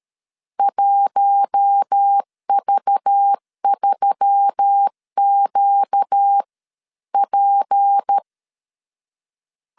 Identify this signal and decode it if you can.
Morse code, heard as a keyed tone, about 13 wpm, spaced as for 11 wpm, 800 Hz. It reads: 1V3QP
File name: CW / DET / 1 6 1500C S B 12wpm 3431